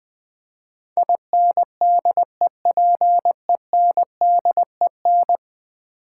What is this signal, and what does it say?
Morse code, heard as a keyed tone, about 20 wpm, 700 Hz. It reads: INDEPENDEN
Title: Build A Word - 5 Letters At A Time - 12-Letter Words - 20wpm